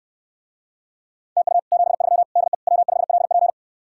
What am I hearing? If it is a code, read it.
Morse code, heard as a keyed tone, about 69 wpm, 700 Hz. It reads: TU 73 DE W5LP